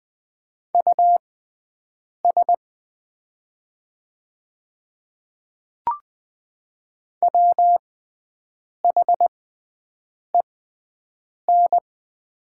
Morse code, heard as a keyed tone, about 20 wpm, spaced as for 7 wpm, 700 Hz. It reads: US WHEN
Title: 6x Character Spacing - Top 200 Words - 20wpm